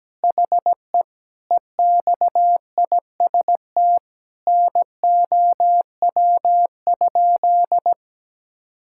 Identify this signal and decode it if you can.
Morse code, heard as a keyed tone, about 17 wpm, 700 Hz. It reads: HE EXIST NOW?
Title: Sentences from Top 1000 Words - Review Part 5 - 17wpm